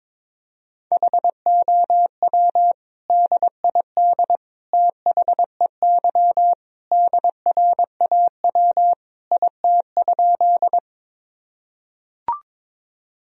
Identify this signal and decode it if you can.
Morse code, heard as a keyed tone, about 21 wpm, 700 Hz. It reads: HOW DID THEY DRAW IT?
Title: Sentences from Top 1000 Words - Review Part 1 - 22wpm